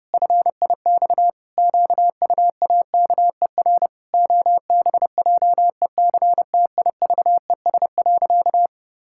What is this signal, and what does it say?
Morse code, heard as a keyed tone, about 32 wpm, 700 Hz. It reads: FIX QUAKER OBJECTIVES.
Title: Pangram Challenge v2 - 30wpm